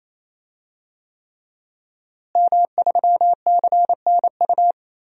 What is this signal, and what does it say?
Morse code, heard as a keyed tone, about 28 wpm, 700 Hz. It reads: M3CNU
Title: DXCC-100 Call Signs - 5 Characters - Repeated 3x - 28wpm